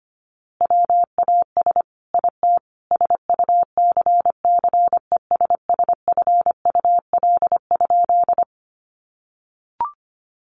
Morse code, heard as a keyed tone, about 25 wpm, 700 Hz. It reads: WAS IT SUCCESSFUL?
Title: A Sound Of Thunder 25wpm